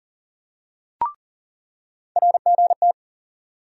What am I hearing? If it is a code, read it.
Morse code, heard as a keyed tone, about 40 wpm, 700 Hz. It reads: RGT
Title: USA Call Sign Postfixes - 3 Letters 40wpm